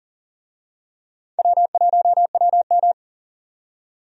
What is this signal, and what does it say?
Morse code, heard as a keyed tone, about 40 wpm, 700 Hz. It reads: W1WM
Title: USA Call Signs - 4 Characters - 40wpm